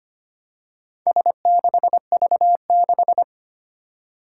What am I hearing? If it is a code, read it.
Morse code, heard as a keyed tone, about 25 wpm, 700 Hz. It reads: S6V6
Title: Four Challenging Letter-Number Combinations 25wpm